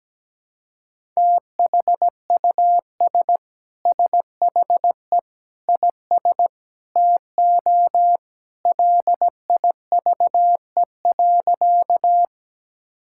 Morse code, heard as a keyed tone, about 17 wpm, 700 Hz. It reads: THUS SHE IS TO LIVE.
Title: Sentences from Top 1000 Words - Review Part 5 - 17wpm